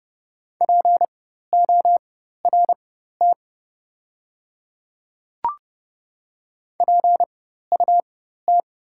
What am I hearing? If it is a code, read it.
Morse code, heard as a keyed tone, about 30 wpm, spaced as for 14 wpm, 700 Hz. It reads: PORT PUT
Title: 4x Character Spacing - Top 200 Words - 30wpm